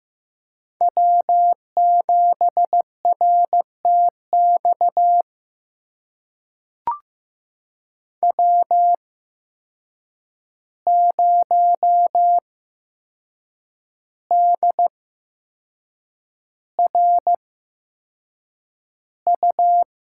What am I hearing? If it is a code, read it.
Morse code, heard as a keyed tone, about 15 wpm, spaced as for 6 wpm, 700 Hz. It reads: W7RTX W0DRU
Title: USA Call Signs - 5 Characters - 8x Character Spacing - 15wpm